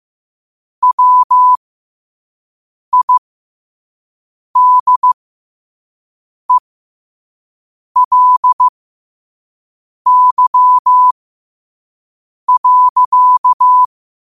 Morse code, heard as a keyed tone, about 15 wpm, spaced as for 5 wpm, 1000 Hz. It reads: WIDELY.
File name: qotdfiles / 5wpm